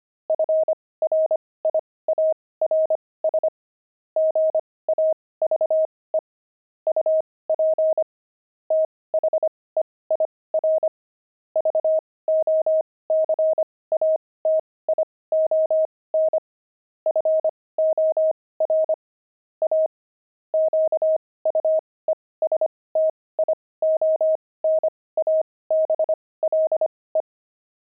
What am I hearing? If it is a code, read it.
Morse code, heard as a keyed tone, about 25 wpm, spaced as for 18 wpm, 630 Hz. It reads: FRIARS GAVE UP THEIR VOCATION FOR A QUESTIONABLE
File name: pangram-fs-18-cs-25-friars